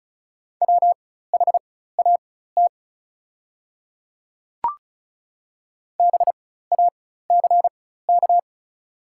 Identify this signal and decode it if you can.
Morse code, heard as a keyed tone, about 36 wpm, spaced as for 16 wpm, 690 Hz. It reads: WHAT BACK
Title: 4x Character Spacing - Top 200 Words - 35wpm